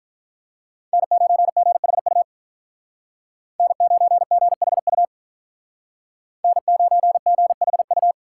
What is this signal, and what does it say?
Morse code, heard as a keyed tone, about 45 wpm, 700 Hz. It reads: N9GHU N9GHU N9GHU